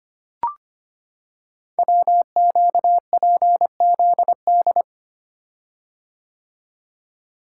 Morse code, heard as a keyed tone, about 25 wpm, 700 Hz. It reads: WQPZD
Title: Code Groups - 5 Letter:Numbers - 25wpm